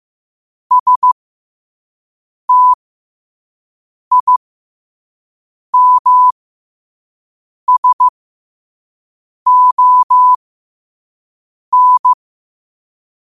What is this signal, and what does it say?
Morse code, heard as a keyed tone, about 15 wpm, spaced as for 5 wpm, 1000 Hz. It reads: STIMSON
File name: qotdfiles / 5wpm